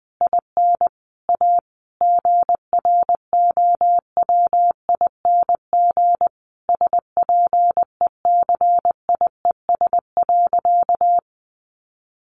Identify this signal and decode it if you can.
Morse code, heard as a keyed tone, about 20 wpm, 700 Hz. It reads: IN A GROWING SPECIES.
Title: A Sound Of Thunder 20wpm